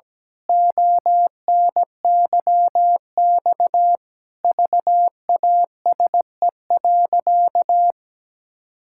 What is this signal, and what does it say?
Morse code, heard as a keyed tone, about 17 wpm, 700 Hz. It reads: ONYX VASE.
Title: Pangram Challenge v2 - 17wpm